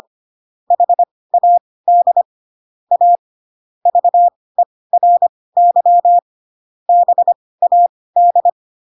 Morse code, heard as a keyed tone, about 25 wpm, spaced as for 18 wpm, 700 Hz. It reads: HAD A VERY BAD
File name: Pinocchio / Pinocchio ch2 12 25 ik1hge